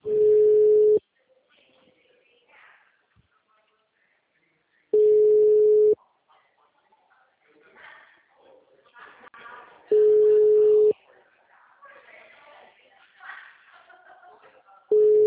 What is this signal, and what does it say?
Ringback tone (Europe, 425 Hz), with speech.